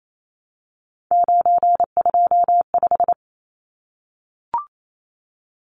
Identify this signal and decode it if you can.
Morse code, heard as a keyed tone, about 28 wpm, 700 Hz. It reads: 925
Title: Numbers - 3-Digit 28wpm